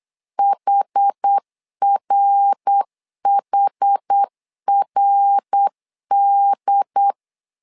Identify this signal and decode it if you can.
Morse code, heard as a keyed tone, about 8 wpm, 800 Hz. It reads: HRHRD
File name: 4 Q 8wpm